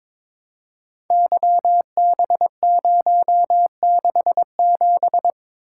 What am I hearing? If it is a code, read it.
Morse code, heard as a keyed tone, about 22 wpm, 690 Hz. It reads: YB067
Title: Code Groups - 5 Letter:Numbers - 22wpm